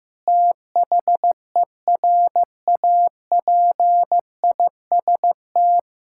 Morse code, heard as a keyed tone, about 15 wpm, 700 Hz. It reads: THERAPIST
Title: K1USN SST QSO Practice 15wpm